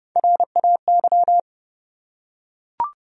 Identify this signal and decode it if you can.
Morse code, heard as a keyed tone, about 32 wpm, 700 Hz. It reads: RAY
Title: USA Common Names - Max 4 Letters - 30wpm